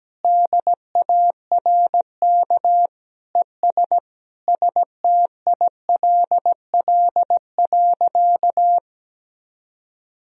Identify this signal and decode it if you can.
Morse code, heard as a keyed tone, about 17 wpm, 700 Hz. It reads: DARK ES STILL.
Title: Sentences from Top 500 Words and Abbreviations 17wpm